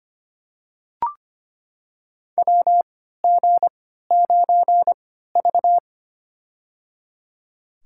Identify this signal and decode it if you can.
Morse code, heard as a keyed tone, about 25 wpm, spaced as for 14 wpm, 700 Hz. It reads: WG9V
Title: USA Call Signs - 4 Characters - 3x Character Spacing - 25wpm